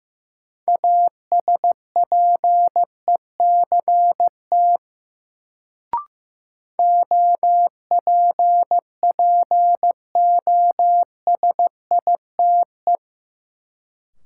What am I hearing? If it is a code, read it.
Morse code, heard as a keyed tone, about 15 wpm, 700 Hz. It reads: ASPECT OPPOSITE
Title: Sentences from Top 1300 Words - New Word Drill - 15wpm